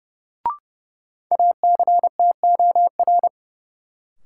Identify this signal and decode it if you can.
Morse code, heard as a keyed tone, about 30 wpm, 700 Hz. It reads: ACTOR
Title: Sentences from Top 1500 Words - New Word Drill Alt - 30wpm